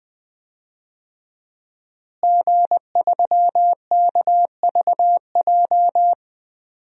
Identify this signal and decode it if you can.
Morse code, heard as a keyed tone, about 20 wpm, 700 Hz. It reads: G3KVJ
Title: DXCC-100 Call Signs - 5 Characters - Repeated 2x - 20wpm